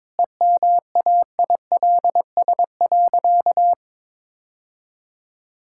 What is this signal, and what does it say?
Morse code, heard as a keyed tone, about 22 wpm, 700 Hz. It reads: EMAILS.
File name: practice / 22wpm / Easy Sentences with Hints - 22wpm